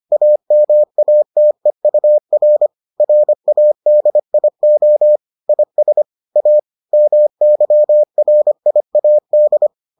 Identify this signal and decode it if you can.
Morse code, heard as a keyed tone, about 25 wpm, 600 Hz. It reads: AMATEUR RADIO IS A MYRIAD